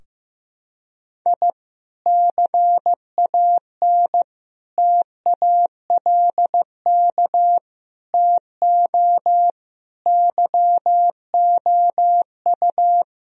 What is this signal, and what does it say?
Morse code, heard as a keyed tone, about 15 wpm, 700 Hz. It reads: I CAN TALK TO YOU
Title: Sentences from Top 1000 Words - Review Part 4 - 15wpm